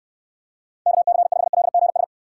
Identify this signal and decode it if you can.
Morse code, heard as a keyed tone, about 69 wpm, 700 Hz. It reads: KC5LGS